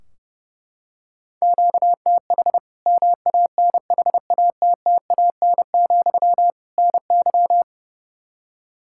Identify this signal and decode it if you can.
Morse code, heard as a keyed tone, about 32 wpm, 700 Hz. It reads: QTH MANHATTAN, NY
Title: Practice QSO - Exchange 1 - Beginning - 30wpm